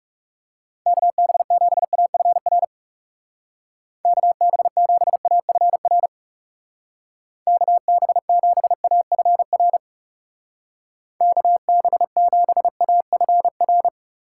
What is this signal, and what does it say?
Morse code, heard as a keyed tone, about 37 wpm, 700 Hz. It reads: KB7AFR KB7AFR KB7AFR KB7AFR
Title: Speed Racing - USA 2x3 Call Signs 30wpm